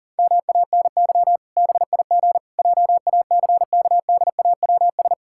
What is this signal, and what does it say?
Morse code, heard as a keyed tone, about 40 wpm, 700 Hz. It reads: MANY BIG JACKDAWS
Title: Pangram Challenge v2 - 40wpm